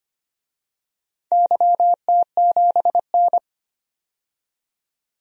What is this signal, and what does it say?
Morse code, heard as a keyed tone, about 25 wpm, 700 Hz. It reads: YT7N
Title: DXCC-100 Call Signs - 4 Characters - 25wpm